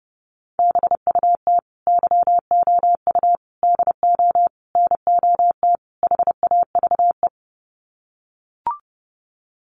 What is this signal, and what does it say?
Morse code, heard as a keyed tone, about 30 wpm, 700 Hz. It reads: BUT YOU DO NOT HAVE
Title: Sets of 5 Words Encore 30wpm